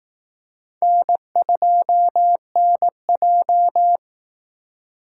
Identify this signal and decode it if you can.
Morse code, heard as a keyed tone, about 18 wpm, 700 Hz. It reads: N2NJ
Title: Speed Racing - USA 1x2 Call Signs 15wpm